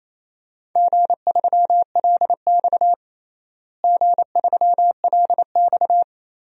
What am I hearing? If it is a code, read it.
Morse code, heard as a keyed tone, about 28 wpm, 700 Hz. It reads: G3LX G3LX
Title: DXCC-100 Call Signs - 4 Characters - Repeated 3x - 28wpm